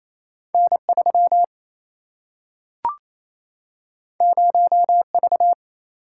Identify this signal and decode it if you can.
Morse code, heard as a keyed tone, about 28 wpm, 700 Hz. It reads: N 3 0 V
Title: Single-Letter Single-Number 28wpm